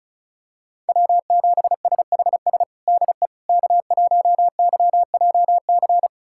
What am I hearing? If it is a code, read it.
Morse code, heard as a keyed tone, about 36 wpm, 700 Hz. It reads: W7SHS DE K1YJC
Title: USA Callsign DE Callsign - 5 Characters - 35wpm